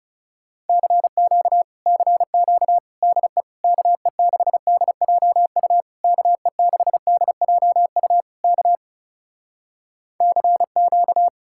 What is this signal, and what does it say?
Morse code, heard as a keyed tone, about 37 wpm, 700 Hz. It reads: CQ CQ DE KE6DJU KE6DJU K CQ
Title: Speed-Racing Common QSO Phrases - 30wpm